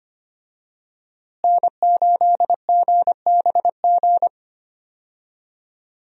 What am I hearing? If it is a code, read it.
Morse code, heard as a keyed tone, about 25 wpm, 700 Hz. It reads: N8GBG
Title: USA Call Signs - 5 Characters - Repeated 3x - 25wpm